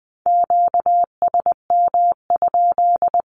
Morse code, heard as a keyed tone, about 20 wpm, 700 Hz. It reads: QSM?
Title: Q-Codes 20wpm